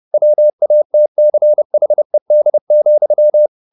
Morse code, heard as a keyed tone, about 30 wpm, 600 Hz. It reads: WATCHED,
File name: the old man and the sea / the old man and the sea 1 30wpm